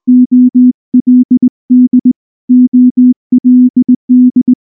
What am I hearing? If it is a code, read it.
Morse code, heard as a keyed tone, about 18 wpm, 250 Hz. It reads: OLD OLD
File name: morse-training-20WPM